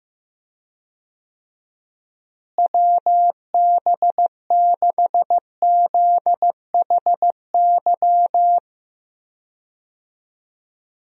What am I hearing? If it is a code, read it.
Morse code, heard as a keyed tone, about 15 wpm, 700 Hz. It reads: WB6ZHY
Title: USA Call Signs - 6 Characters - Repeated 3x - 15wpm